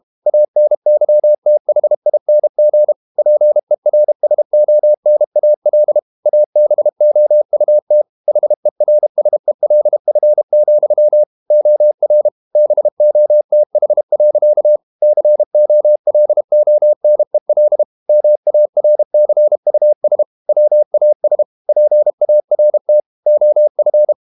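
Morse code, heard as a keyed tone, about 32 wpm, 600 Hz. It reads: ANYTHING PERSONAL ABOUT HERSELF, OR BOTH. COLONEL MARCUS WAS PART OF
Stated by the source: Morse code practice at 32 wpm from CW med Gnister, reading Ken Follett's novel Never in English.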